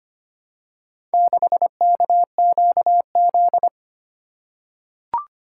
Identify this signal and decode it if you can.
Morse code, heard as a keyed tone, about 25 wpm, 700 Hz. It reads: 6KQZ
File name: practice / 25wpm / Four Letter-Number Combinations - 25wpm